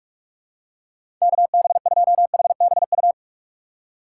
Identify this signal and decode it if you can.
Morse code, heard as a keyed tone, about 46 wpm, 700 Hz. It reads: KB2HBU
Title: USA Call Signs - 6 Characters - 2x Character Spacing - 45wpm